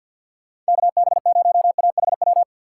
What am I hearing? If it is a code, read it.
Morse code, heard as a keyed tone, about 59 wpm, 700 Hz. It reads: KB0AHW
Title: Speed Racing - USA Call Signs 50wpm